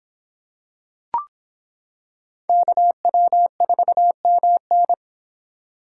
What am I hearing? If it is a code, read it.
Morse code, heard as a keyed tone, about 26 wpm, 700 Hz. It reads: KW4MN